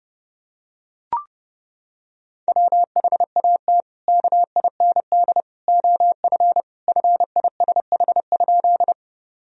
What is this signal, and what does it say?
Morse code, heard as a keyed tone, about 30 wpm, 700 Hz. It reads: WHAT KIND OF FISH?
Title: Sentences from Top 1000 Words - Review Part 1 - 30wpm